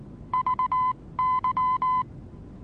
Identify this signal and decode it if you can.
Morse code, heard as a keyed tone, about 19 wpm, spaced as for 16 wpm, 1020 Hz. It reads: VY
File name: repos / Morsekurs / DJ4UF Morsen26